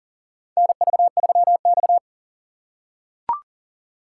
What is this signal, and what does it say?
Morse code, heard as a keyed tone, about 42 wpm, 690 Hz. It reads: NV3X
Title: USA Call Signs - 4 Characters - 2x Character Spacing - 40wpm